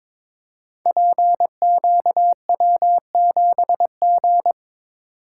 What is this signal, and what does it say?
Morse code, heard as a keyed tone, about 22 wpm, 700 Hz. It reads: PQW7G
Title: Code Groups - 5 Letter:Numbers - 22wpm